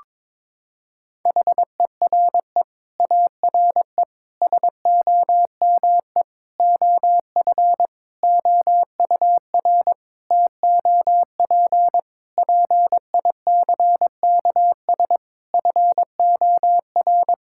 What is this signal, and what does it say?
Morse code, heard as a keyed tone, about 22 wpm, 700 Hz. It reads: HERE ARE SOME OF OUR TOP PICKS FOR
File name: Sentences with 9-Word Phrases - 22wpm